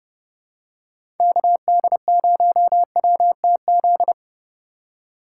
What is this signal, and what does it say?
Morse code, heard as a keyed tone, about 30 wpm, 700 Hz. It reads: KD0WTZ
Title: USA Call Signs - 6 Characters - 3x Character Spacing - 30wpm